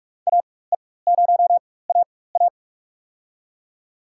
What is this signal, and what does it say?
Morse code, heard as a keyed tone, about 42 wpm, spaced as for 21 wpm, 700 Hz. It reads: AE0AA
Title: USA Call Signs - 5 Characters - 4x Character Spacing - 45wpm